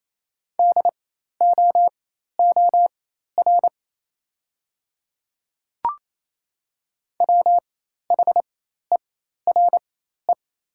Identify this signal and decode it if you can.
Morse code, heard as a keyed tone, about 28 wpm, spaced as for 13 wpm, 700 Hz. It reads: DOOR WHERE
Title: 4x Character Spacing - Top 300 Words - 28wpm